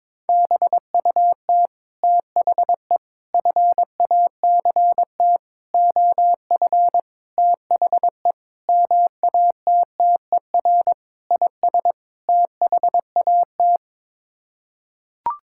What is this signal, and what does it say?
Morse code, heard as a keyed tone, about 22 wpm, 700 Hz. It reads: BUT THE FACT OF THE MATTER IS THAT
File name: practice / 22wpm / Sets of 8 Words - 22wpm